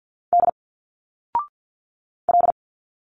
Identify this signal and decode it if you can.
Morse code, heard as a keyed tone, about 51 wpm, 700 Hz. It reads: D L
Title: Single Letters 50wpm